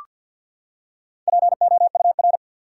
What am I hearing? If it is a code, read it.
Morse code, heard as a keyed tone, about 47 wpm, 700 Hz. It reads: POUR
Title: Four Letter Words - 50wpm